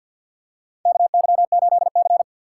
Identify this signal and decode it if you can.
Morse code, heard as a keyed tone, about 47 wpm, 700 Hz. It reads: KY8C
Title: USA Call Signs - 4 Characters - 6x Character Spacing - 50wpm